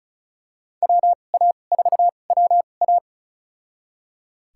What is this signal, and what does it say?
Morse code, heard as a keyed tone, about 34 wpm, spaced as for 25 wpm, 700 Hz. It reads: WA4WA
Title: USA Call Signs - 5 Characters - 2x Character Spacing - 35wpm